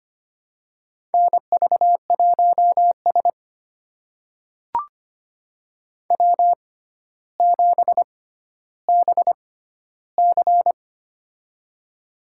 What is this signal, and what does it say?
Morse code, heard as a keyed tone, about 25 wpm, spaced as for 13 wpm, 700 Hz. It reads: NV1S W7BC
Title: USA Call Signs - 4 Characters - 6x Character Spacing - 25wpm